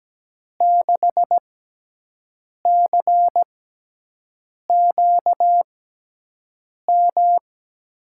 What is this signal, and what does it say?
Morse code, heard as a keyed tone, about 17 wpm, spaced as for 6 wpm, 700 Hz. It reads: 6CQM
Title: USA Call Signs - 6x Character Spacing - 17wpm